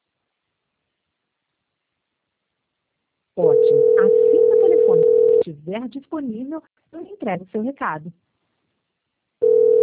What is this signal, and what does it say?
Ringback tone (North America), 440 + 480 Hz, with speech.
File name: out- -2009- - - .5292